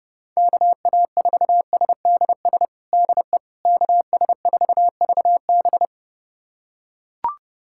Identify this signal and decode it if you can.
Morse code, heard as a keyed tone, about 30 wpm, 700 Hz. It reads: KA4SDS DE KS4VB
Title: USA Callsign DE Callsign 30wpm